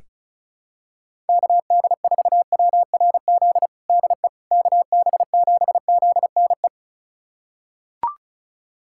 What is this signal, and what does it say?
Morse code, heard as a keyed tone, about 34 wpm, 700 Hz. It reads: KD4WRZ DE KB7ZNE
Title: USA Callsign DE Callsign - 6 Characters - 35wpm